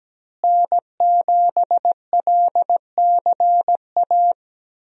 Morse code, heard as a keyed tone, about 17 wpm, 700 Hz. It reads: N7LCA
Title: USA Call Signs - 5 Characters - Repeated 2x - 17wpm